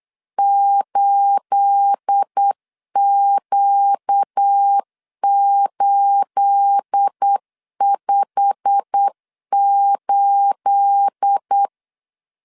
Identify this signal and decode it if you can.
Morse code, heard as a keyed tone, about 8 wpm, 800 Hz. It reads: 8Q858
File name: LEZIONI / 9 28 8wpm 3512